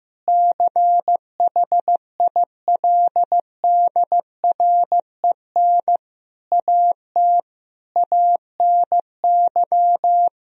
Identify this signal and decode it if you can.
Morse code, heard as a keyed tone, about 15 wpm, 700 Hz. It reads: CHILDREN AT ANY
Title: Sentences from Top 1000 Words - Review Part 3 - 15wpm